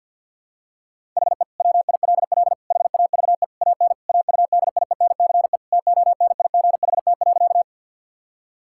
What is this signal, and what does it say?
Morse code, heard as a keyed tone, about 50 wpm, 700 Hz. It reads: HE WILL HAVE AN AUDIENCE TONIGHT.
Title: Sentences from Top 1000 Words - Review Part 5 - 50wpm